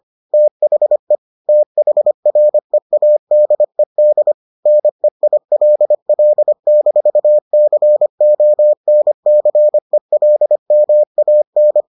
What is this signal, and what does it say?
Morse code, heard as a keyed tone, about 25 wpm, 600 Hz. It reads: THE THREADED NEILL-CONCELMAN